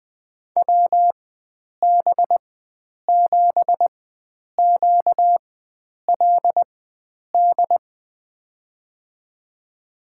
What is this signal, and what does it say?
Morse code, heard as a keyed tone, about 20 wpm, spaced as for 9 wpm, 700 Hz. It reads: WB7QLD
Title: USA Call Signs - 4x Character Spacing - 20wpm